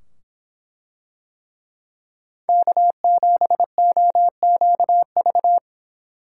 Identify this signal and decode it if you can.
Morse code, heard as a keyed tone, about 24 wpm, 700 Hz. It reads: K7OQV